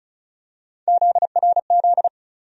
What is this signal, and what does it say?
Morse code, heard as a keyed tone, about 34 wpm, 700 Hz. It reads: ZRZ
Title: USA Call Sign Postfixes - 3 Letters 35wpm